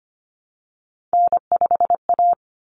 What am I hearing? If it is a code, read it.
Morse code, heard as a keyed tone, about 25 wpm, 700 Hz. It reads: N5A
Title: Common Cut Numbers - 3-Digit 25wpm